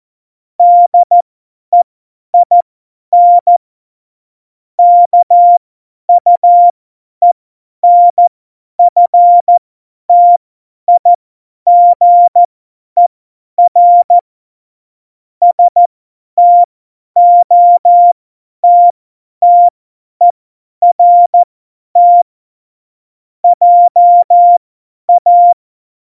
Morse code, heard as a keyed tone, about 14 wpm, spaced as for 10 wpm, 700 Hz. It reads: DEIN KUENFTIGER STOTTERT JA
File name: Morsekurs / Witze / Witz 27. August 2019, 50 F70